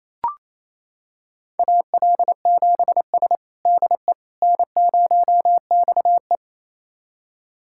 Morse code, heard as a keyed tone, about 28 wpm, 700 Hz. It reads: AL7S DE N0XE
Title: USA Callsign DE Callsign - 4 Characters - 28wpm